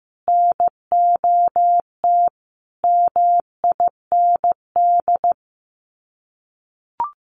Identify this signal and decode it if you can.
Morse code, heard as a keyed tone, about 15 wpm, 700 Hz. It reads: NOT MIND
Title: Sets of 5 Words 15wpm